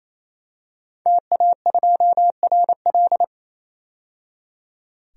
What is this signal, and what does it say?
Morse code, heard as a keyed tone, about 28 wpm, 700 Hz. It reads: TA2RL
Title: DXCC-100 Call Signs - 5 Characters - 28wpm